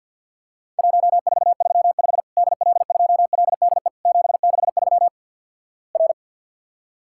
Morse code, heard as a keyed tone, about 52 wpm, 700 Hz. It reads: 1435 DL2LDE 763 R
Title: Worked All Europe DX Contest - QTC Exchange - 50wpm